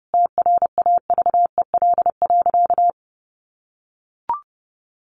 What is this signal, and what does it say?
Morse code, heard as a keyed tone, about 30 wpm, 700 Hz. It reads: TRAVEL.
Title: Sentences from Top 1300 Words 30wpm